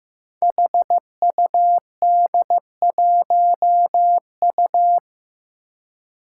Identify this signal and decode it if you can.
Morse code, heard as a keyed tone, about 15 wpm, 700 Hz. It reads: HUD1U
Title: Five Challenging Letter-Number Combinations 15wpm